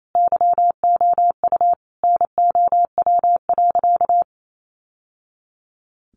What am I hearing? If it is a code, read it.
Morse code, heard as a keyed tone, about 28 wpm, 700 Hz. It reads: YOU NOW.
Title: Sentences from Top 1200 Words 28wpm